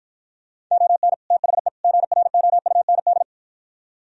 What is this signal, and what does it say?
Morse code, heard as a keyed tone, about 52 wpm, 700 Hz. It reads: ON THE GROUND